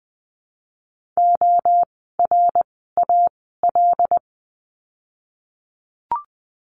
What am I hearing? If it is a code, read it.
Morse code, heard as a keyed tone, about 20 wpm, 700 Hz. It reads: O R A L
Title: 075 Lesson 13f - Letter L - Words 20wpm